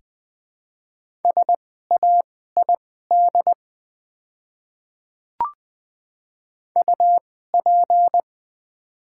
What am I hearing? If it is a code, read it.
Morse code, heard as a keyed tone, about 20 wpm, spaced as for 14 wpm, 700 Hz. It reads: SAID UP
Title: 2x Character Spacing - Top 200 Words - 20wpm